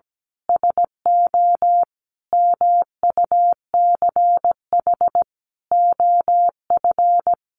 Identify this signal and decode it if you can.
Morse code, heard as a keyed tone, about 17 wpm, 700 Hz. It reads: SO MUCH OF
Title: Sentences from Top 1400 Words 17wpm